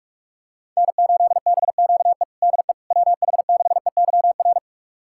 Morse code, heard as a keyed tone, about 46 wpm, 700 Hz. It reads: N8BQE DE WH6EYR
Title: USA Callsign DE Callsign 45wpm